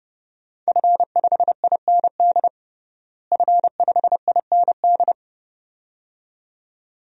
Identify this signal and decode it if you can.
Morse code, heard as a keyed tone, about 29 wpm, 700 Hz. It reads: F5IND F5IND
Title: DXCC-100 Call Signs - 5 Characters - Repeated 2x - 30wpm